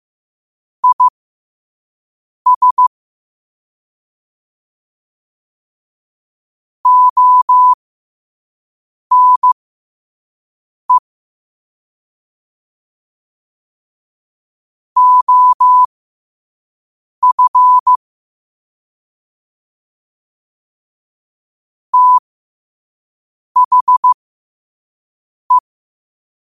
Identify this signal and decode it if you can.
Morse code, heard as a keyed tone, about 15 wpm, spaced as for 5 wpm, 1000 Hz. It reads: IS ONE OF THE
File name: qotdfiles / 5wpm